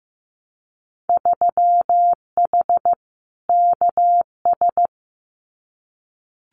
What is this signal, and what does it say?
Morse code, heard as a keyed tone, about 15 wpm, 700 Hz. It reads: 3H KS
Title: Winter Field Day Exchanges 15wpm